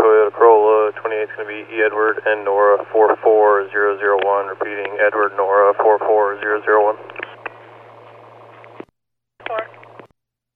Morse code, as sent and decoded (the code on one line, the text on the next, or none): none